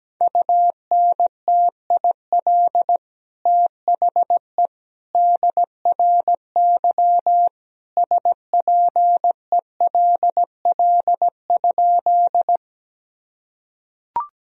Morse code, heard as a keyed tone, about 17 wpm, 700 Hz. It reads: UNTIL THE DRY SPELL?
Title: Sentences from Top 1000 Words - Review Part 3 - 17wpm